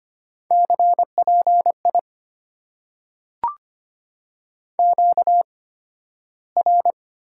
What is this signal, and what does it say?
Morse code, heard as a keyed tone, about 25 wpm, spaced as for 13 wpm, 690 Hz. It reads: CPI QR
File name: Top 90 QSO Elements - 8x Character Spacing - 25wpm